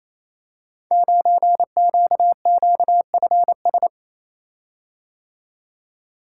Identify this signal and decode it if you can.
Morse code, heard as a keyed tone, about 28 wpm, 700 Hz. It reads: 9QQFS
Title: Code Groups - 5 Letter:Numbers - 28wpm